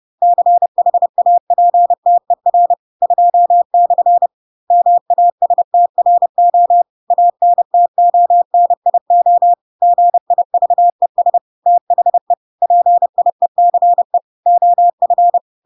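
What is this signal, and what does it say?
Morse code, heard as a keyed tone, about 30 wpm, 700 Hz. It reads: CHAPTER 2/ MASTRO ANTONIO GIVES THE PIECE OF
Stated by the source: Chapter 2 of The Adventures of Pinocchio read as a book in Morse code (CW) at 30 wpm, by ik1hge.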